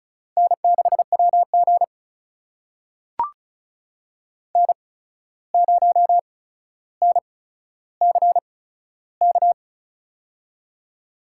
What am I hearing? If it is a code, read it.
Morse code, heard as a keyed tone, about 36 wpm, spaced as for 14 wpm, 700 Hz. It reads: N6WG N0NCK
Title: USA Call Signs - 8x Character Spacing - 35wpm